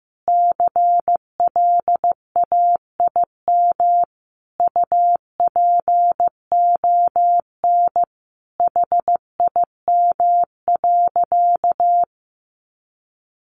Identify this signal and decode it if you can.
Morse code, heard as a keyed tone, about 15 wpm, 700 Hz. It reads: CLAIM UPON HIM.
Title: Sentences from Top 1500 Words 15wpm